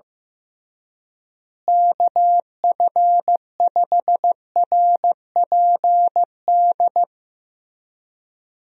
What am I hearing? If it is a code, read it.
Morse code, heard as a keyed tone, about 15 wpm, 700 Hz. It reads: KF5RPD
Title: USA Call Signs - 6 Characters - Repeated 3x - 15wpm